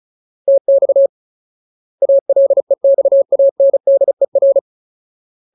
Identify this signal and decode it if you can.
Morse code, heard as a keyed tone, about 34 wpm, 550 Hz. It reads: TX ALEXANDER
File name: Ultimate Word List / 35WPM Ultimate